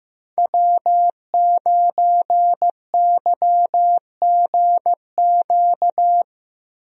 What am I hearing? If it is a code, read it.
Morse code, heard as a keyed tone, about 15 wpm, 700 Hz. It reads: W9YGQ
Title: USA Call Signs - 5 Characters - Repeated 3x - 15wpm